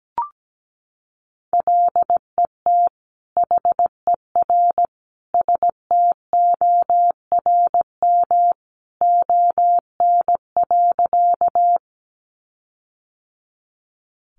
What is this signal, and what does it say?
Morse code, heard as a keyed tone, about 17 wpm, 700 Hz. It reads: LET HER STORM ON.
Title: Sentences from Top 1500 Words 17wpm